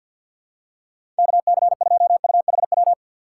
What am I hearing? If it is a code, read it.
Morse code, heard as a keyed tone, about 48 wpm, 700 Hz. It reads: KC2UHW
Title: USA Call Signs - 4x Character Spacing - 50wpm